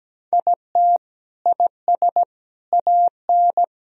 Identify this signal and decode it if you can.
Morse code, heard as a keyed tone, about 17 wpm, 700 Hz. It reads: IT IS AN
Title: Sentences from Top 1000 Words - Review Part 4 - 17wpm